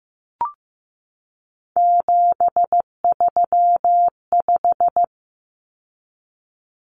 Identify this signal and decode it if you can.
Morse code, heard as a keyed tone, about 15 wpm, 700 Hz. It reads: 735
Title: Common Cut Numbers - 3-Digit 15wpm